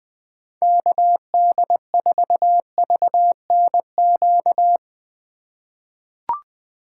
Morse code, heard as a keyed tone, about 20 wpm, 700 Hz. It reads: KD4VNQ
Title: USA Call Signs - 3x Character Spacing - 20wpm